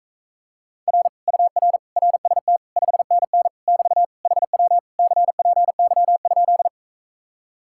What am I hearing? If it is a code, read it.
Morse code, heard as a keyed tone, about 45 wpm, 700 Hz. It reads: R UR RST 5NN <BT> HW CPY?